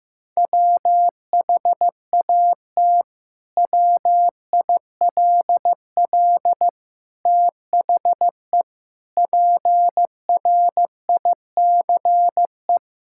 Morse code, heard as a keyed tone, about 15 wpm, 700 Hz. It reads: WHAT WILL THE PRICE
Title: Sentences from Top 1000 Words - Review Part 5 - 15wpm